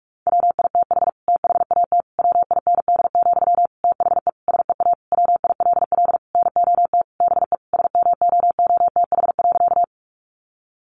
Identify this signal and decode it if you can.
Morse code, heard as a keyed tone, about 44 wpm, 700 Hz. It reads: WITH THAT WIND, THE SEA WILL NOT BE SMOOTH.
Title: Sentences from Top 1400 Words 45wpm